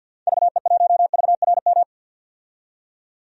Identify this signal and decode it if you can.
Morse code, heard as a keyed tone, about 52 wpm, 700 Hz. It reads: VE1VRM
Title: DXCC-100 Call Signs - 6 Characters - 50wpm